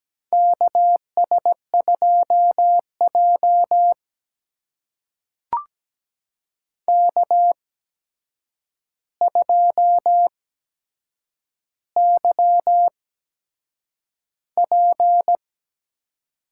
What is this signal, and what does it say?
Morse code, heard as a keyed tone, about 17 wpm, spaced as for 7 wpm, 700 Hz. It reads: KS2J K2YP
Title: USA Call Signs - 4 Characters - 8x Character Spacing - 17wpm